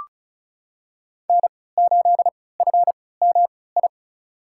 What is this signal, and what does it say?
Morse code, heard as a keyed tone, about 34 wpm, spaced as for 20 wpm, 700 Hz. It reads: N8FMI
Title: USA Call Signs - 5 Characters - 3x Character Spacing - 35wpm